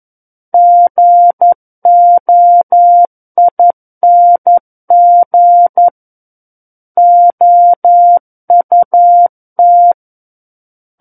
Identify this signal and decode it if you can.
Morse code, heard as a keyed tone, about 11 wpm, 700 Hz. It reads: GOING OUT
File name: 11 WPM